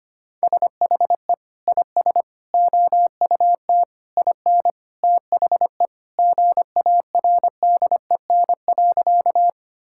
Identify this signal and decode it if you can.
Morse code, heard as a keyed tone, about 25 wpm, 700 Hz. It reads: SHE IS OUT IN THE GARDEN.
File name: practice / 25wpm / Sentences with 3-Word Phrases - 25wpm